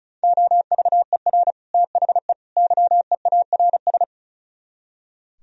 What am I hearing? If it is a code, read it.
Morse code, heard as a keyed tone, about 44 wpm, 700 Hz. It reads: OVER THE YEARS
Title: Speed-Racing - Sets of 3 Words 35wpm